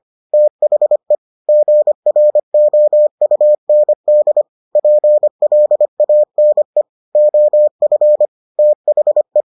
Morse code, heard as a keyed tone, about 25 wpm, 600 Hz. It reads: THE GROUND PLANE OF THE